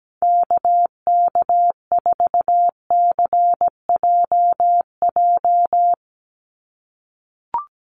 Morse code, heard as a keyed tone, about 17 wpm, 700 Hz. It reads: KK4CJJ
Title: USA Call Signs 17wpm